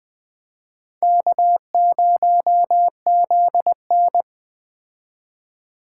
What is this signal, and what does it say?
Morse code, heard as a keyed tone, about 20 wpm, 700 Hz. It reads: K0ZN